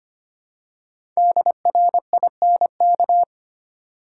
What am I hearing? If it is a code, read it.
Morse code, heard as a keyed tone, about 25 wpm, 700 Hz. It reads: DRINK
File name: 5 Letter Words - First 3 Letters Given - 25wpm